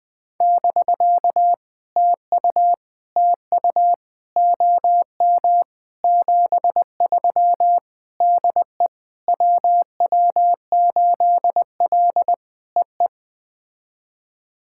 Morse code, heard as a keyed tone, about 20 wpm, 700 Hz. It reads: <BK> TU TU OM 73 DE WW8L EE
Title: POTA QSO Practice - 20wpm